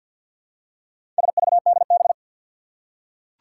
Morse code, heard as a keyed tone, about 47 wpm, 700 Hz. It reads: SVDB
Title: Four Challenging Letter-Number Combinations 50wpm